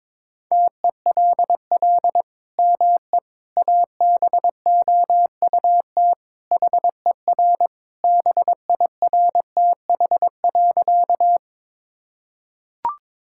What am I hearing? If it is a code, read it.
Morse code, heard as a keyed tone, about 22 wpm, 700 Hz. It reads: TELL ME ABOUT HER BIRTH.
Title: Sentences from Top 1000 Words - Review Part 5 - 22wpm